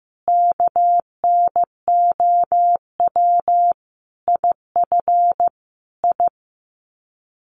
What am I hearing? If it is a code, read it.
Morse code, heard as a keyed tone, about 15 wpm, 700 Hz. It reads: KNOW IF I
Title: Sets of 5 Words 15wpm